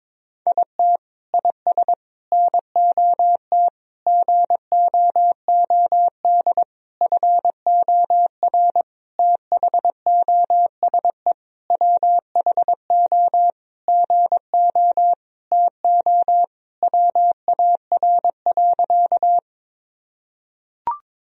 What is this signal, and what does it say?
Morse code, heard as a keyed tone, about 22 wpm, 700 Hz. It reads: IT IS NOT GOOD FOR THOSE WHO GO TO WAR.
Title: Sentences from Top 1000 Words - Review Part 2 - 22wpm